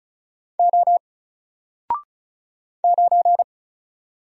Morse code, heard as a keyed tone, about 36 wpm, 700 Hz. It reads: O 9
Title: Single Letter-Number Alt 35wpm